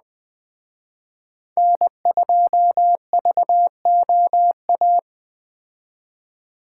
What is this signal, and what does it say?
Morse code, heard as a keyed tone, about 19 wpm, 700 Hz. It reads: N2VOA